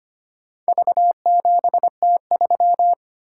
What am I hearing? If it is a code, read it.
Morse code, heard as a keyed tone, about 25 wpm, 700 Hz. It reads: V7T3
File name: practice / 25wpm / Four Letter-Number Combinations - 25wpm